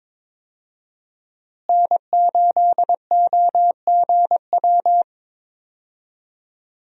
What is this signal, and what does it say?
Morse code, heard as a keyed tone, about 21 wpm, 700 Hz. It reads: N8OGW
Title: USA Call Signs - 5 Characters - Repeated 2x - 22wpm